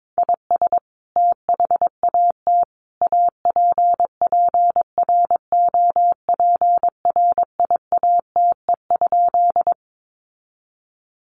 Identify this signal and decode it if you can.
Morse code, heard as a keyed tone, about 22 wpm, 700 Hz. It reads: IS THAT APPROPRIATE?
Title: Sentences from Top 900 Words 22wpm